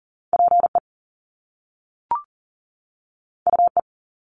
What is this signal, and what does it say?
Morse code, heard as a keyed tone, about 38 wpm, 700 Hz. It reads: PE UE
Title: Top Two Letter Combinations in Words 40wpm